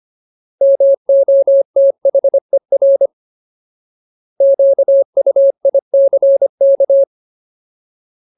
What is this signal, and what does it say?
Morse code, heard as a keyed tone, about 25 wpm, 550 Hz. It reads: MOTHER QUICK
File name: Ultimate Word List / 25WPM Ultimate